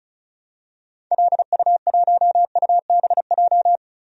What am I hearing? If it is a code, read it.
Morse code, heard as a keyed tone, about 36 wpm, 700 Hz. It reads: LU1UBJ
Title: DXCC-100 Call Signs - 6 Characters - 35wpm